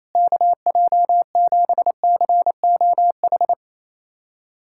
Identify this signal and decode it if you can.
Morse code, heard as a keyed tone, about 28 wpm, 700 Hz. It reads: KJ7COH
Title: USA Callsign DE Callsign 28wpm